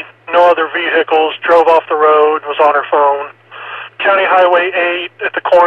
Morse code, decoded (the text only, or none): none